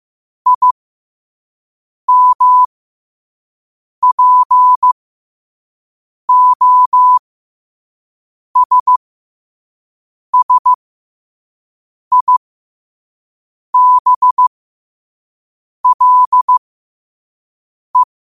Morse code, heard as a keyed tone, about 15 wpm, spaced as for 5 wpm, 1000 Hz. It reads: IMPOSSIBLE
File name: qotdfiles / 5wpm